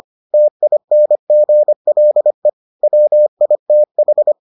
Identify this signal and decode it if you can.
Morse code, heard as a keyed tone, about 25 wpm, 600 Hz. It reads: TINGLE WITH